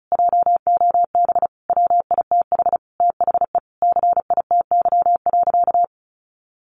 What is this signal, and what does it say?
Morse code, heard as a keyed tone, about 34 wpm, 700 Hz. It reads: JOB WITH THE CITY.
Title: Sentences from Top 1500 Words 35wpm